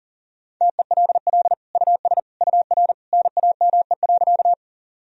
Morse code, heard as a keyed tone, about 40 wpm, 700 Hz. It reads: TELL US UR NAME.
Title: Sentences from Top 200 Words and Abbreviations 40wpm